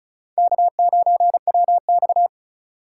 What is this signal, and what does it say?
Morse code, heard as a keyed tone, about 34 wpm, 700 Hz. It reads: K9WX
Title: USA Call Signs - 4 Characters - 6x Character Spacing - 35wpm